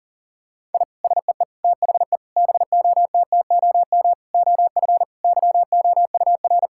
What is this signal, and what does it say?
Morse code, heard as a keyed tone, about 37 wpm, 700 Hz. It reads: I SEE THE BOTTOM OF YOUR
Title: Sentences from Top 1000 Words - Review Part 6 - 40wpm